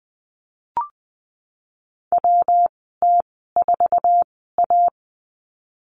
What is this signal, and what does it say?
Morse code, heard as a keyed tone, about 20 wpm, spaced as for 15 wpm, 700 Hz. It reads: WT4A
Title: 147 Lesson 24g - Letter y - Call Signs 20wpm